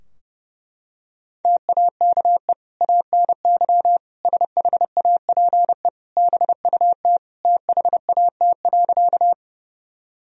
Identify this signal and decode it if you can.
Morse code, heard as a keyed tone, about 30 wpm, 700 Hz. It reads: TAKE ANY SHAPE BUT THAT.
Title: Sentences from Top 1000 Words - Review Part 3 - 30wpm